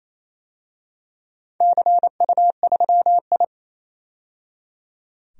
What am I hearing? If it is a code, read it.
Morse code, heard as a keyed tone, about 27 wpm, 690 Hz. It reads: CU3I
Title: DXCC-100 Call Signs - 4 Characters - 28wpm